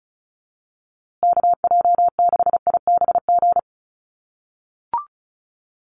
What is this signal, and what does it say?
Morse code, heard as a keyed tone, about 34 wpm, 700 Hz. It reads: KJ6IBG
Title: USA Call Signs Repeated 2x 35wpm